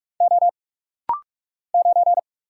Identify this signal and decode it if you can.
Morse code, heard as a keyed tone, about 46 wpm, 700 Hz. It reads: O 9
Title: Single Letter-Number - Rapid-Fire 45wpm